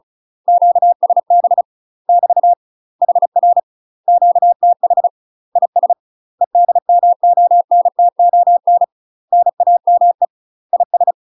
Morse code, heard as a keyed tone, about 34 wpm, 700 Hz. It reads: QSB <BT> HR QTH IS EDMONTON NAME IS
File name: UK MASTER QSOs TRAINING / 35WPM / UK MASTER QSOs 0008